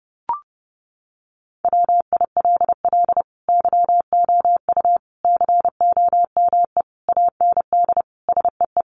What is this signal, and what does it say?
Morse code, heard as a keyed tone, about 28 wpm, 700 Hz. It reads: WILL YOU COME AND SEE
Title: Sentences from Top 100 Words 30wpm